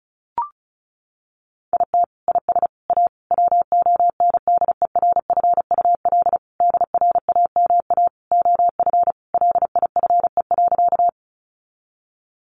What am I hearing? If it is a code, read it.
Morse code, heard as a keyed tone, about 34 wpm, 700 Hz. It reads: IT IS A WONDERFUL DRAMA OF LIFE.
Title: Sentences from Top 1500 Words 35wpm